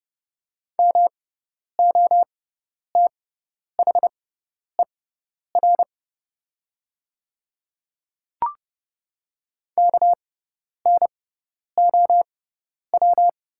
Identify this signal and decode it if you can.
Morse code, heard as a keyed tone, about 29 wpm, spaced as for 10 wpm, 690 Hz. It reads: MOTHER KNOW
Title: 6x Character Spacing - Top 200 Words - 30wpm